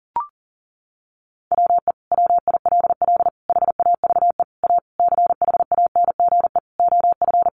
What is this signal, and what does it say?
Morse code, heard as a keyed tone, about 37 wpm, 700 Hz. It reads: WE WILL HAVE A CHANGE OF
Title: Sentences from Top 1400 Words 40wpm